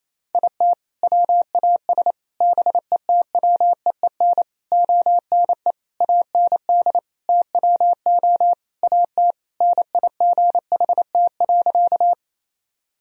Morse code, heard as a keyed tone, about 28 wpm, 700 Hz. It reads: IT WAS BETWEEN ONE AND TWO AT NIGHT.
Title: Sentences from Top 1000 Words - Review Part 1 - 28wpm